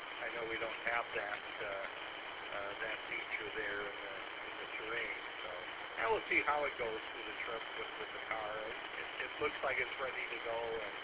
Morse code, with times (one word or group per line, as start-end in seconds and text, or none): none